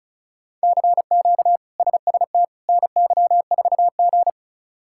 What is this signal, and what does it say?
Morse code, heard as a keyed tone, about 36 wpm, 700 Hz. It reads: CQ SST NY4G
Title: K1USN SST QSO Practice - 35wpm